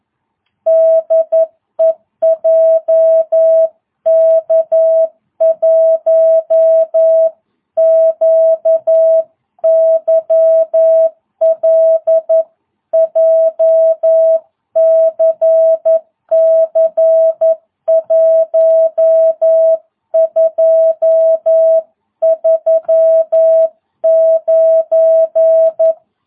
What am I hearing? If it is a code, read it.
Morse code, heard as a keyed tone, about 11 wpm, 650 Hz. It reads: DE JK1QYL JCC 1239